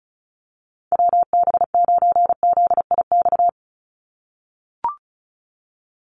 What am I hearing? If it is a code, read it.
Morse code, heard as a keyed tone, about 36 wpm, 700 Hz. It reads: WB9ZIX
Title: USA Call Signs 35wpm